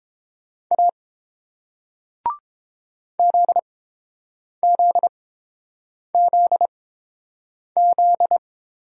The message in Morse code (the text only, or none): A ZZZZ